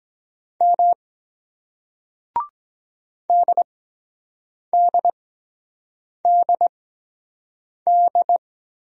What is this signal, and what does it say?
Morse code, heard as a keyed tone, about 23 wpm, spaced as for 7 wpm, 700 Hz. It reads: M DDDD